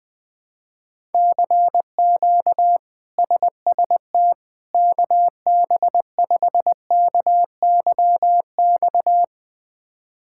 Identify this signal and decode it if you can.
Morse code, heard as a keyed tone, about 21 wpm, 700 Hz. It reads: CQ SST KB5KYX
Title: K1USN SST QSO Practice - 20wpm